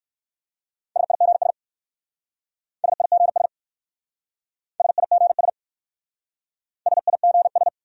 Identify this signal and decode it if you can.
Morse code, heard as a keyed tone, about 54 wpm, 700 Hz. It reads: SIGS SIGS SIGS SIGS
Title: Speed-Racing Abbreviations - 45wpm